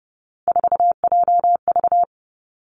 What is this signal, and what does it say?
Morse code, heard as a keyed tone, about 31 wpm, 700 Hz. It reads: 4JV
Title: Three Challenging Letter-Number Combinations 30wpm